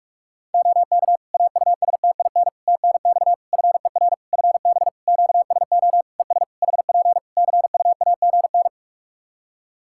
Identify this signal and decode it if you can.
Morse code, heard as a keyed tone, about 46 wpm, 700 Hz. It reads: OK AUSTIN TNX FER FB QSO ES HP CUAGN
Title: Practice QSO - Exchange 4 - Ending - 45wpm